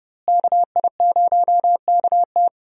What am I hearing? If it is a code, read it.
Morse code, heard as a keyed tone, about 30 wpm, 700 Hz. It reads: KI0KT
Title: USA Call Signs - 5 Characters - Repeated 3x - 30wpm